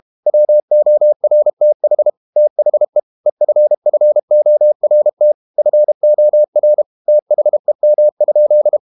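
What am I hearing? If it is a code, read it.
Morse code, heard as a keyed tone, about 32 wpm, 600 Hz. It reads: WORTH THE EFFORT FOR THEM?
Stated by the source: Morse code practice at 32 wpm from CW med Gnister, reading Ken Follett's novel Never in English.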